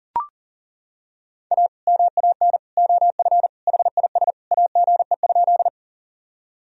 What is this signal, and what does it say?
Morse code, heard as a keyed tone, about 37 wpm, 700 Hz. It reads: A MAN OF HIS AGE?
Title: Sentences from Top 1000 Words - Review Part 3 - 40wpm